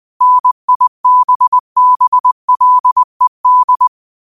Morse code, heard as a keyed tone, about 20 wpm, 1000 Hz. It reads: NIBBLED
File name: qotdfiles / 20wpm